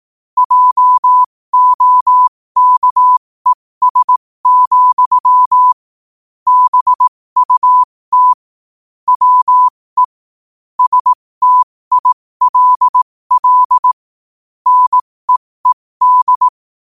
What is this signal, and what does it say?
Morse code, heard as a keyed tone, about 18 wpm, spaced as for 15 wpm, 1000 Hz. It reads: JOKES, BUT WE STILL NEED
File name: qotdfiles / 15wpm